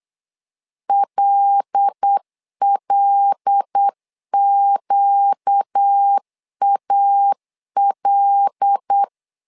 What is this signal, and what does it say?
Morse code, heard as a keyed tone, about 8 wpm, 800 Hz. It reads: LLQAL
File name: LEZIONI / 5 LBVC / 5 L 8wpm 2831